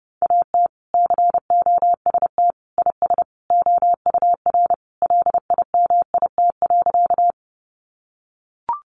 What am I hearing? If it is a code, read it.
Morse code, heard as a keyed tone, about 30 wpm, 700 Hz. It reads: AT COST IS OUR LIMIT.
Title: Sentences from Top 1500 Words 30wpm